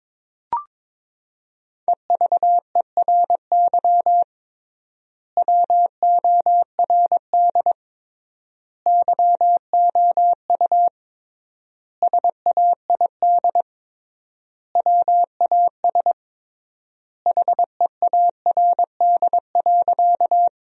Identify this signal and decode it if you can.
Morse code, heard as a keyed tone, about 22 wpm, 700 Hz. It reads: EVERY WORD YOU SAID WAS HEARD.
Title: 3x Word-Spacing - Sentences from Top 400 Words - 22wpm